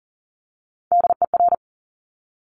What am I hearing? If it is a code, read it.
Morse code, heard as a keyed tone, about 38 wpm, 690 Hz. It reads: DER
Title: Top Three Letter Combinations in Words 40wpm